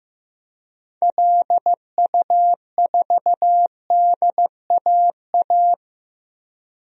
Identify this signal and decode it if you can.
Morse code, heard as a keyed tone, about 16 wpm, 700 Hz. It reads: LU4DAA
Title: DXCC-100 Call Signs - 6 Characters - Repeated 3x - 15wpm